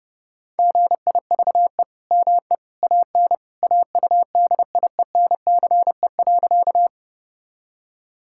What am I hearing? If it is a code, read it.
Morse code, heard as a keyed tone, about 30 wpm, 700 Hz. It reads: GIVE ME AN AUDIENCE.
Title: Sentences from Top 1000 Words - Review Part 5 - 30wpm